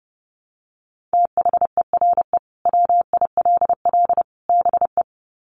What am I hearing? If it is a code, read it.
Morse code, heard as a keyed tone, about 30 wpm, 700 Hz. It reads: THERE WILL BE
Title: Sets of 3 Words 30wpm